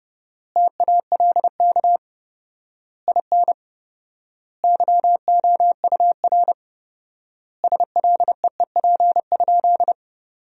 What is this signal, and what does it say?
Morse code, heard as a keyed tone, about 31 wpm, 700 Hz. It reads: TALK IN YOUR SLEEP?
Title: 4x Word-Spacing - Sentences from Top 500 Words - 30wpm